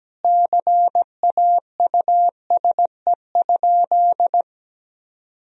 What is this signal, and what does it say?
Morse code, heard as a keyed tone, about 17 wpm, 700 Hz. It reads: CAUSE?
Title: Sentences from Top 1000 Words - Review Part 1 - 17wpm